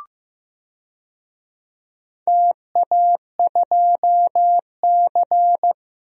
Morse code, heard as a keyed tone, about 15 wpm, 700 Hz. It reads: TA2C
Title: DXCC-100 Call Signs - 4 Characters - 15wpm